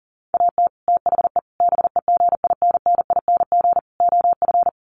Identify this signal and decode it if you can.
Morse code, heard as a keyed tone, about 40 wpm, 700 Hz. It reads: AT THE BEGINNING OF
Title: Sets of 4 Words 40wpm